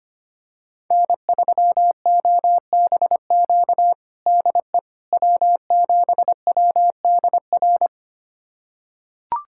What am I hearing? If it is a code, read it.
Morse code, heard as a keyed tone, about 25 wpm, 700 Hz. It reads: N3OBQ DE W7WDR
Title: USA Callsign DE Callsign - 5 Characters - 25wpm